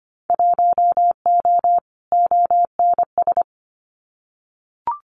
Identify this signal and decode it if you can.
Morse code, heard as a keyed tone, about 25 wpm, 700 Hz. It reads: 1O ONS
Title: Winter Field Day Exchanges 25wpm